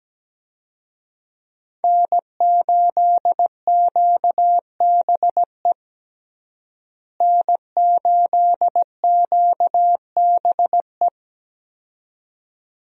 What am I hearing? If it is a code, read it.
Morse code, heard as a keyed tone, about 17 wpm, 700 Hz. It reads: N8QBE N8QBE
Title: USA Call Signs - 5 Characters - Repeated 2x - 17wpm